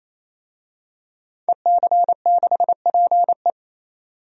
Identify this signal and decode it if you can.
Morse code, heard as a keyed tone, about 28 wpm, 700 Hz. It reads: EC6PE
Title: DXCC-100 Call Signs - 5 Characters - 28wpm